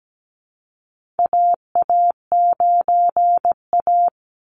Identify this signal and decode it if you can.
Morse code, heard as a keyed tone, about 17 wpm, 700 Hz. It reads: AA9A
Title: Call Signs - 17wpm